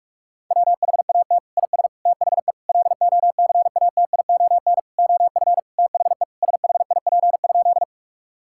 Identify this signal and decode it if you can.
Morse code, heard as a keyed tone, about 41 wpm, 700 Hz. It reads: WHAT IS THE LOCATION OF THE SHIP?
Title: Sentences from Top 1000 Words - Review Part 6 - 45wpm